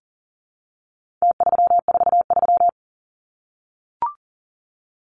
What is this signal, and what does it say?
Morse code, heard as a keyed tone, about 40 wpm, 700 Hz. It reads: T343
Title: Numbers - Contest Style - 4-Digit 40wpm